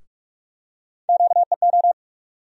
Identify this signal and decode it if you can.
Morse code, heard as a keyed tone, about 47 wpm, 700 Hz. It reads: QEO
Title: USA Call Sign Postfixes - 3 Letters 45wpm